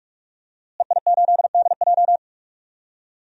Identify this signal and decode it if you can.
Morse code, heard as a keyed tone, about 41 wpm, 700 Hz. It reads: EI8DJ
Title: DXCC-100 Call Signs - 5 Characters - Repeated 2x - 45wpm